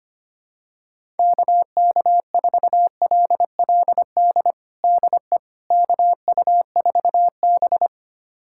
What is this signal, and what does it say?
Morse code, heard as a keyed tone, about 25 wpm, 700 Hz. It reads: KK4LLD DE KU4B
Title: USA Callsign DE Callsign 25wpm